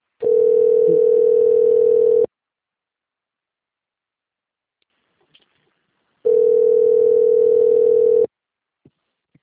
Ringback tone (North America), 440 + 480 Hz.